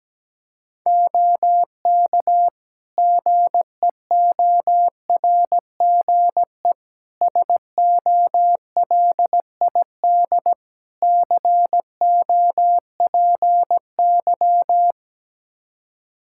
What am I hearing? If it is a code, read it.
Morse code, heard as a keyed tone, about 17 wpm, 700 Hz. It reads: OK GEORGE SOLID COPY
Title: K1USN SST QSO Practice 17wpm